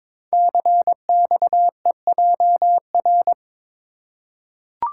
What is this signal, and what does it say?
Morse code, heard as a keyed tone, about 22 wpm, 690 Hz. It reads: CXEJR
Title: Code Groups - 5 Letter:Numbers - 22wpm